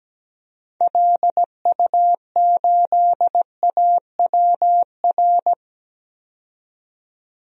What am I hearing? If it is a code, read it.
Morse code, heard as a keyed tone, about 17 wpm, 700 Hz. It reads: LU8AWR
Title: DXCC-100 Call Signs - 6 Characters - 17wpm